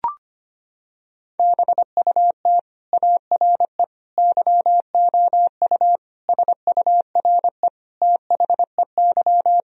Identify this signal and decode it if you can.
Morse code, heard as a keyed tone, about 25 wpm, 700 Hz. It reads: BUT ARE YOU SURE THEY
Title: Sentences from Top 1000 Words - Review Part 1 - 25wpm